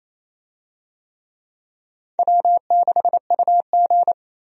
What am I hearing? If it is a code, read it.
Morse code, heard as a keyed tone, about 28 wpm, 700 Hz. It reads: W6UG
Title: USA Call Signs - 4 Characters - 28wpm